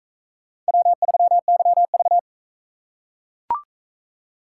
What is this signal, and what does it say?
Morse code, heard as a keyed tone, about 41 wpm, 700 Hz. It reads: W3YV